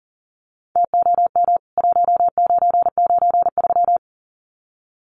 Morse code, heard as a keyed tone, about 40 wpm, 700 Hz. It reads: TOM 1993
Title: CWT Exchange 40wpm